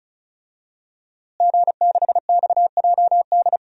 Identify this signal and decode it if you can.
Morse code, heard as a keyed tone, about 36 wpm, 700 Hz. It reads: G6XJD
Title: DXCC-100 Call Signs - 5 Characters - Repeated 2x - 35wpm